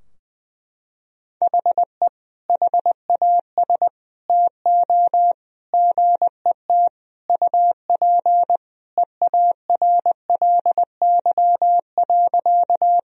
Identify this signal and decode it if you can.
Morse code, heard as a keyed tone, about 20 wpm, 700 Hz. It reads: HE HAS TO GET UP EARLY.
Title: Sentences from Top 1000 Words - Review Part 2 - 20wpm